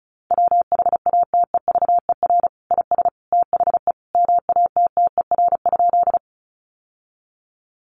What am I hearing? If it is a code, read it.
Morse code, heard as a keyed tone, about 34 wpm, 700 Hz. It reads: WHATEVER IS THE MATTER?
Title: Sentences from Top 1100 Words 35wpm